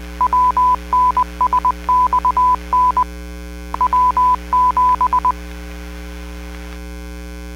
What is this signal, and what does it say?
Morse code, heard as a keyed tone, about 20 wpm, 1000 Hz. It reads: WNSXN W7
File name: 451.25